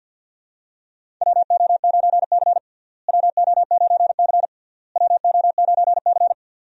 Morse code, heard as a keyed tone, about 52 wpm, 700 Hz. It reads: WO9C WO9C WO9C
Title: USA Call Signs - 4 Characters - Repeated 3x - 50wpm